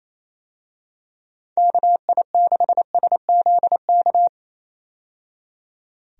Morse code, heard as a keyed tone, about 28 wpm, 700 Hz. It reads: KI6SZK
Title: USA Call Signs - 6 Characters - 28wpm